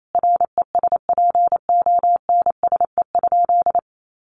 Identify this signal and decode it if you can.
Morse code, heard as a keyed tone, about 28 wpm, 700 Hz. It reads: RESPONSE?
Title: Sentences from Top 1400 Words 28wpm